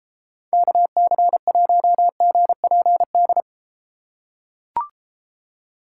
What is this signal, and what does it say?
Morse code, heard as a keyed tone, about 33 wpm, 700 Hz. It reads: KC1GPD